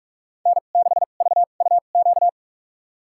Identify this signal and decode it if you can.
Morse code, heard as a keyed tone, about 46 wpm, spaced as for 33 wpm, 700 Hz. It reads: N6VUQ
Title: USA Call Signs - 2x Character Spacing - 45wpm